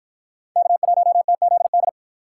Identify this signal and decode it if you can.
Morse code, heard as a keyed tone, about 53 wpm, 700 Hz. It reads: K1TZD